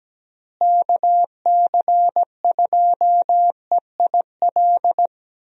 Morse code, heard as a keyed tone, about 17 wpm, 700 Hz. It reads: KC2EIL
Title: USA Call Signs - 6 Characters - Repeated 2x - 17wpm